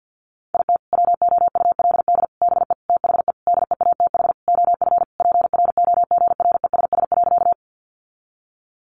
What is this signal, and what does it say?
Morse code, heard as a keyed tone, about 50 wpm, 700 Hz. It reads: IT WOULD BE THE DEATH OF PROGRESS.
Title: Sentences from Top 1400 Words 50wpm